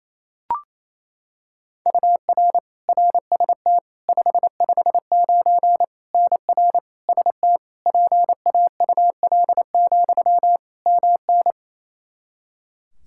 Morse code, heard as a keyed tone, about 27 wpm, 700 Hz. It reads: UR RST 559 NR ST PAUL, MN
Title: Practice QSO - Exchange 1 - Beginning - 28wpm